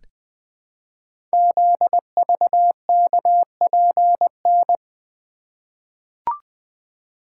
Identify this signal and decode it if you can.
Morse code, heard as a keyed tone, about 20 wpm, 700 Hz. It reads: ZVKPN
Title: Code Groups - 5 Letters - 20wpm